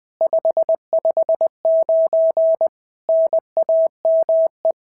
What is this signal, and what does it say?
Morse code, heard as a keyed tone, about 20 wpm, 690 Hz. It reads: 559 NAME
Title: SOTA QSO Practice 20wpm